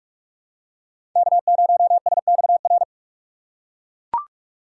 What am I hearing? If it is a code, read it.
Morse code, heard as a keyed tone, about 43 wpm, 700 Hz. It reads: K0SXR
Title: USA Call Signs - 5 Characters - 4x Character Spacing - 45wpm